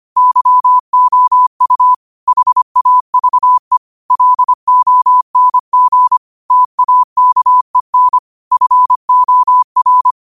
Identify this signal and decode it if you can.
Morse code, heard as a keyed tone, about 25 wpm, 1000 Hz. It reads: YOU HAVE LONG TAKEN FOR
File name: qotdfiles / 25wpm